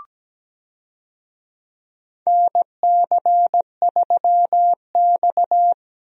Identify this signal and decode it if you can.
Morse code, heard as a keyed tone, about 17 wpm, 700 Hz. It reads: NC3X
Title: USA Call Signs - 4 Characters - Repeated 2x - 17wpm